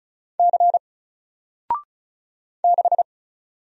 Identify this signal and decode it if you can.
Morse code, heard as a keyed tone, about 36 wpm, 700 Hz. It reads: C 6
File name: Single-Letter-Number-35wpm